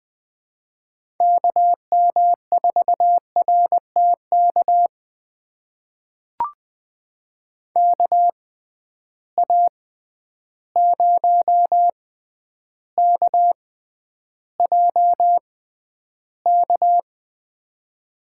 Morse code, heard as a keyed tone, about 21 wpm, spaced as for 10 wpm, 700 Hz. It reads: KM4RTK KA0KJK
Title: USA Call Signs - 6 Characters - 6x Character Spacing - 20wpm